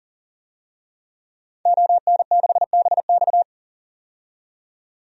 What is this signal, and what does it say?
Morse code, heard as a keyed tone, about 40 wpm, 700 Hz. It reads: ON6BX
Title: DXCC-100 Call Signs - 5 Characters - 40wpm